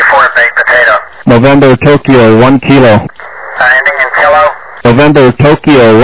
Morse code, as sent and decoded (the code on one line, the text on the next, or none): none